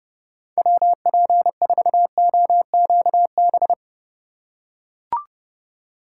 Morse code, heard as a keyed tone, about 29 wpm, 700 Hz. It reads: WP4OQB